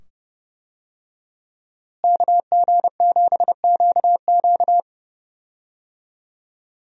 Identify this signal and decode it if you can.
Morse code, heard as a keyed tone, about 30 wpm, 700 Hz. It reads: KG7QQ
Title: USA Call Signs - 5 Characters - Repeated 2x - 30wpm